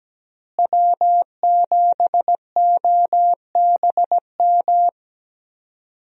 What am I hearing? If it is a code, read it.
Morse code, heard as a keyed tone, about 17 wpm, 700 Hz. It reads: W7OBM
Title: USA Call Signs - 5 Characters - Repeated 2x - 17wpm